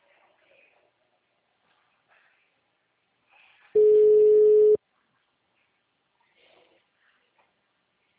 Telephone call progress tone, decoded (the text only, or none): ringback tone